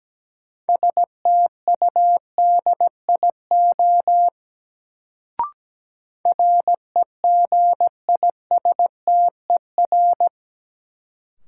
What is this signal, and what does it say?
Morse code, heard as a keyed tone, about 17 wpm, 700 Hz. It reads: STUDIO REGISTER
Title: Sentences from Top 1200 Words - New Word Drill Alt - 17wpm